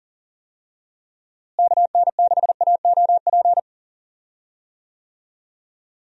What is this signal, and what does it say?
Morse code, heard as a keyed tone, about 40 wpm, 700 Hz. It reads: KN6AOP
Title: USA Call Signs - 6 Characters - Repeated 2x - 40wpm